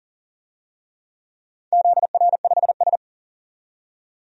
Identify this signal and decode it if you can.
Morse code, heard as a keyed tone, about 40 wpm, 700 Hz. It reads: ZR5S
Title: DXCC-100 Call Signs - 4 Characters - 40wpm